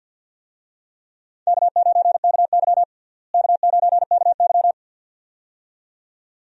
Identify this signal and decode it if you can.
Morse code, heard as a keyed tone, about 52 wpm, 700 Hz. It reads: K9KY K9KY
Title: USA Call Signs - 4 Characters - Repeated 2x - 50wpm